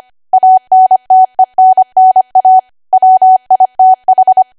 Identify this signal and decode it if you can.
Morse code, heard as a keyed tone, about 25 wpm, 750 Hz. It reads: ANTENNA WITH
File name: Archive / 25 WPM / 25WPM